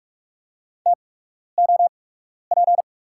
Morse code, heard as a keyed tone, about 46 wpm, spaced as for 12 wpm, 700 Hz. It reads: TOP
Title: 8x Character Spacing - Top 500 Words - 45wpm